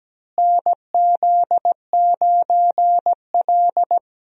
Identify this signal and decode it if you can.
Morse code, heard as a keyed tone, about 17 wpm, 700 Hz. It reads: NZ9L
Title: USA Call Signs - 4 Characters - Repeated 3x - 17wpm